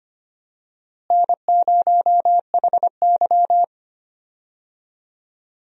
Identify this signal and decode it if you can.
Morse code, heard as a keyed tone, about 25 wpm, 700 Hz. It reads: N0HY
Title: USA Call Signs - 4 Characters - Repeated 3x - 25wpm